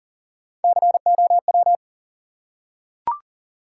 Morse code, heard as a keyed tone, about 42 wpm, 700 Hz. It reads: COW
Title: Three Letter Words - 40wpm